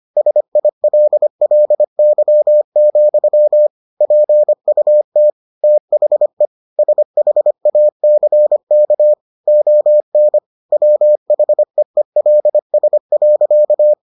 Morse code, heard as a keyed tone, about 25 wpm, 600 Hz. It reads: SILLY, PUT THE SHACK ON WHEELS.